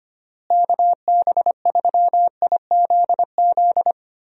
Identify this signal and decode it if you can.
Morse code, heard as a keyed tone, about 25 wpm, 700 Hz. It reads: KB3IZZ
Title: USA Call Signs - 6 Characters - Repeated 2x - 25wpm